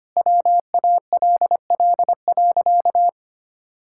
Morse code, heard as a keyed tone, about 25 wpm, 700 Hz. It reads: WALL.
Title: Sentences from Top 1000 Words - Review Part 6 - 25wpm